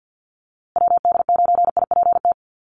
Morse code, heard as a keyed tone, about 50 wpm, 700 Hz. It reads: WD9IPT
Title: USA Call Signs Repeated 3x 50wpm